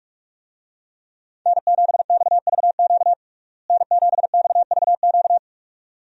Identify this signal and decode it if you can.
Morse code, heard as a keyed tone, about 45 wpm, 700 Hz. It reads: N7XVQ N7XVQ
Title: USA Call Signs - 5 Characters - Repeated 2x - 45wpm